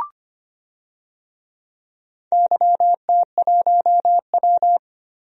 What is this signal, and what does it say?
Morse code, heard as a keyed tone, about 25 wpm, 700 Hz. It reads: YT1W
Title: DXCC-100 Call Signs - 4 Characters - Repeated 2x - 25wpm